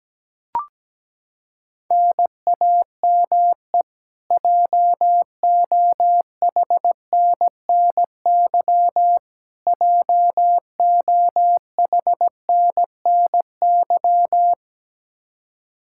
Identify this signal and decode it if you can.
Morse code, heard as a keyed tone, about 17 wpm, 700 Hz. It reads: NAME JOHNNY JOHNNY
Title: Practice QSO - Exchange 1 - Beginning - 17wpm